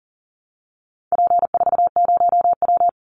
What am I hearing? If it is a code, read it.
Morse code, heard as a keyed tone, about 40 wpm, 700 Hz. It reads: P40W
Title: Call Signs Repeated 3x - 40wpm